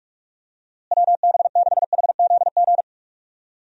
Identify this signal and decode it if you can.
Morse code, heard as a keyed tone, about 46 wpm, 700 Hz. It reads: WB6HZG
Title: USA Call Signs - 6 Characters - Repeated 3x - 45wpm